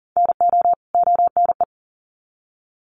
Morse code, heard as a keyed tone, about 40 wpm, 700 Hz. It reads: NO ONE
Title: Sets of 2 Words 40wpm